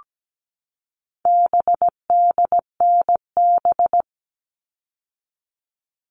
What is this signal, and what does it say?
Morse code, heard as a keyed tone, about 17 wpm, 700 Hz. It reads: BDNB
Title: All Cut Numbers - 4-Digit 17wpm